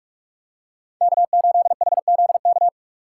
Morse code, heard as a keyed tone, about 46 wpm, 700 Hz. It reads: K8HZK
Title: USA Call Signs - 2x Character Spacing - 45wpm